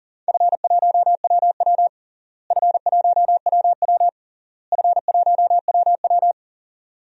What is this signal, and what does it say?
Morse code, heard as a keyed tone, about 40 wpm, 700 Hz. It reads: F1WW F1WW F1WW
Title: DXCC-100 Call Signs - 4 Characters - Repeated 3x - 40wpm